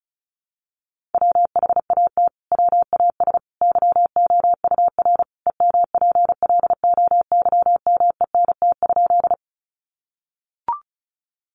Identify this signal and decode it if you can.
Morse code, heard as a keyed tone, about 34 wpm, 700 Hz. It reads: WHAT WAS YOUR EMPLOYMENT?
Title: Sentences from Top 1500 Words 35wpm